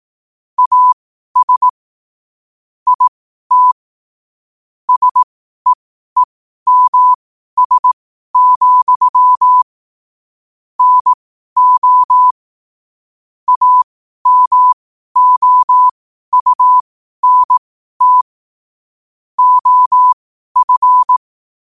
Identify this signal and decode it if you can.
Morse code, heard as a keyed tone, about 18 wpm, spaced as for 12 wpm, 1000 Hz. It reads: AS IT SEEMS, NO AMOUNT OF